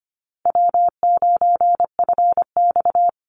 Morse code, heard as a keyed tone, about 25 wpm, 700 Hz. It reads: W9FX
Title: Easy Call Signs Repeated 3x - 25wpm